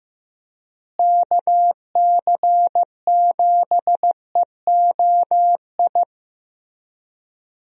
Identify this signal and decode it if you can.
Morse code, heard as a keyed tone, about 15 wpm, 700 Hz. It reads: KC7EOI
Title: USA Call Signs - 6 Characters - Repeated 2x - 15wpm